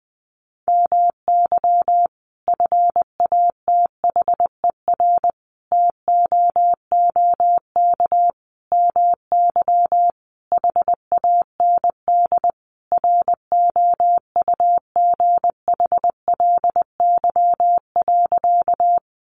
Morse code, heard as a keyed tone, about 20 wpm, 700 Hz. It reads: MY FATHER TOOK MY HAND ROUGHLY.
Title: Sentences from Top 1500 Words 20wpm